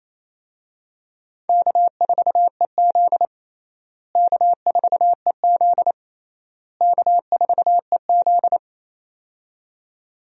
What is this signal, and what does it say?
Morse code, heard as a keyed tone, about 28 wpm, 700 Hz. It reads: K4EZ K4EZ K4EZ
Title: USA Call Signs - 4 Characters - Repeated 3x - 28wpm